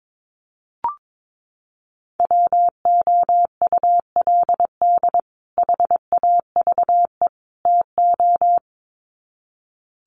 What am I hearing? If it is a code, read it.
Morse code, heard as a keyed tone, about 22 wpm, 700 Hz. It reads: WOULD HAVE TO
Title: Sets of 3 Words 22wpm